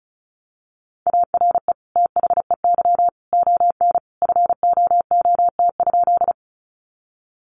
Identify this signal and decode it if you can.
Morse code, heard as a keyed tone, about 34 wpm, 700 Hz. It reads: ARE THEY ON FOOT?
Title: Sentences from Top 500 Words 35wpm